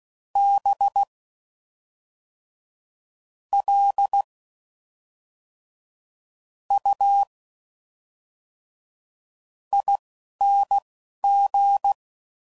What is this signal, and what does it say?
Morse code, heard as a keyed tone, about 16 wpm, spaced as for 5 wpm, 800 Hz. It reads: BLUING
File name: Suffixes / Special INGs / Fast INGs